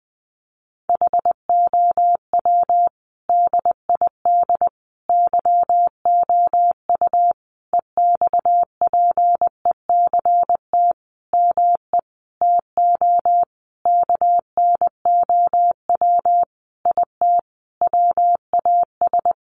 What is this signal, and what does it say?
Morse code, heard as a keyed tone, about 20 wpm, 700 Hz. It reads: HOW DID YOU EXPECT ME TO KNOW IT WAS
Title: Sentences from Top 1100 Words 20wpm